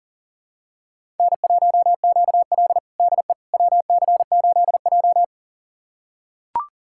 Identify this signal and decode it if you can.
Morse code, heard as a keyed tone, about 37 wpm, 700 Hz. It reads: N1QL DE WC8J
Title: USA Callsign DE Callsign - 4 Characters - 40wpm